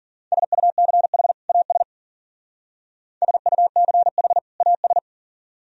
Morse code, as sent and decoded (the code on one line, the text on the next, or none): ... ..- -.-. .... / .- ... / ... ..- -.-. .... / .- ...
SUCH AS SUCH AS